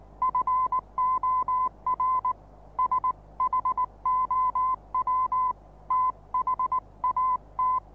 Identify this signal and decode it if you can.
Morse code, heard as a keyed tone, about 19 wpm, 1030 Hz. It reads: FOR SHOW THAT
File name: repos / Morsekurs / DJ4UF Morsen26